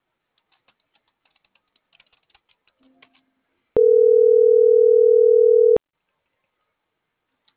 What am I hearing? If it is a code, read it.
Ringback tone (North America), 440 + 480 Hz.